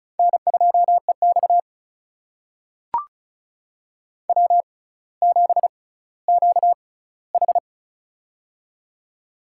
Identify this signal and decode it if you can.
Morse code, heard as a keyed tone, about 36 wpm, spaced as for 18 wpm, 700 Hz. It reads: N2EX W7QH
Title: USA Call Signs - 4 Characters - 6x Character Spacing - 35wpm